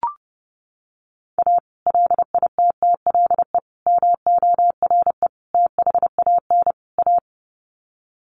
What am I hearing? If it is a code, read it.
Morse code, heard as a keyed tone, about 30 wpm, 700 Hz. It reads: A LITTLE MORE THAN A
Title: Sets of 5 Words Encore2 30wpm